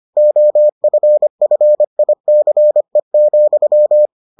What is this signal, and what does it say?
Morse code, heard as a keyed tone, about 25 wpm, 600 Hz. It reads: OFFICE,